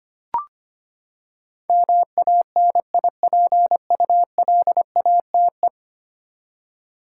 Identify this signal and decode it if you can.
Morse code, heard as a keyed tone, about 25 wpm, 700 Hz. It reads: MANIPULATE
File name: Ten Letter Words - 25wpm